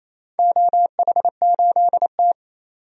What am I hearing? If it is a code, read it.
Morse code, heard as a keyed tone, about 28 wpm, 700 Hz. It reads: OH8T
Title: DXCC-100 Call Signs - 4 Characters - 28wpm